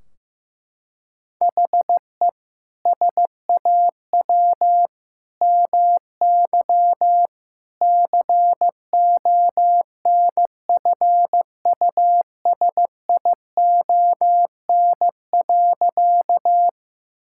Morse code, heard as a keyed tone, about 15 wpm, 700 Hz. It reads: HE SAW MY CONFUSION.
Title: Sentences from Top 1600 Words 15wpm